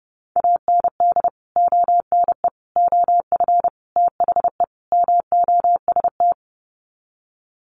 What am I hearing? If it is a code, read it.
Morse code, heard as a keyed tone, about 30 wpm, 700 Hz. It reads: AND ONE OF THE MOST
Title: Sets of 5 Words Encore 30wpm